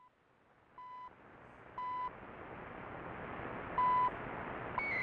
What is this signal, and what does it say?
There is no code.